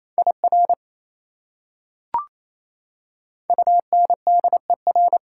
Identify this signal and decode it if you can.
Morse code, heard as a keyed tone, about 28 wpm, 700 Hz. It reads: IR UNDER
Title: Prefixes and Suffixes 28wpm